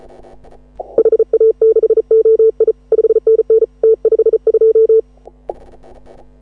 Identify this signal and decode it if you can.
Morse code, heard as a keyed tone, about 34 wpm, 440 Hz. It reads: HA6OI 5NN T52
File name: AUDIO2018 / HA6OI CW